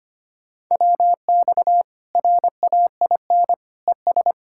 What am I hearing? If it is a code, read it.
Morse code, heard as a keyed tone, about 25 wpm, 700 Hz. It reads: WX RAIN ES
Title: Practice QSO - Exchange 2 - Info - 25wpm